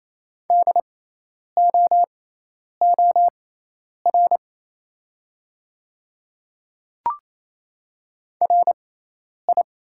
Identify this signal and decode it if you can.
Morse code, heard as a keyed tone, about 27 wpm, spaced as for 10 wpm, 690 Hz. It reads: DOOR RI
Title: 6x Character Spacing - Top 500 Words - 28wpm